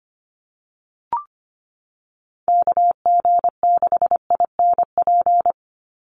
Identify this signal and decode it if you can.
Morse code, heard as a keyed tone, about 25 wpm, 700 Hz. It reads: KG6INP
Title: USA Call Signs 25wpm